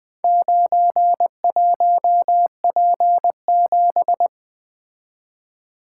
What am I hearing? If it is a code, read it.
Morse code, heard as a keyed tone, about 24 wpm, 700 Hz. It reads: 91P7
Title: Speed-Racing - Four Letter-Number Combinations - 20wpm